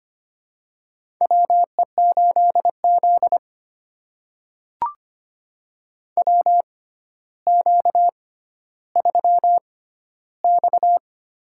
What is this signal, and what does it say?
Morse code, heard as a keyed tone, about 25 wpm, spaced as for 13 wpm, 700 Hz. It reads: WE8Z WQ3X
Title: USA Call Signs - 4 Characters - 6x Character Spacing - 25wpm